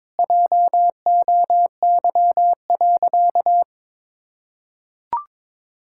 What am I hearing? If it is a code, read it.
Morse code, heard as a keyed tone, about 22 wpm, 700 Hz. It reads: JOY.
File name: practice / 22wpm / Sentences with 2-Word Phrases - 22wpm